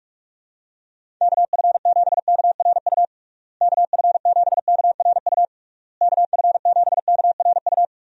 Morse code, heard as a keyed tone, about 46 wpm, 700 Hz. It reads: KF7KRU KF7KRU KF7KRU
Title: USA Call Signs - 6 Characters - Repeated 3x - 45wpm